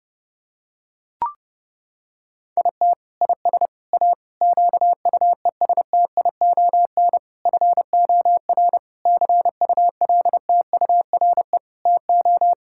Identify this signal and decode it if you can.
Morse code, heard as a keyed tone, about 30 wpm, 700 Hz. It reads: IT IS A QUESTION FOR CULTURE TO
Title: Sentences from Top 1000 Words - Review Part 6 - 30wpm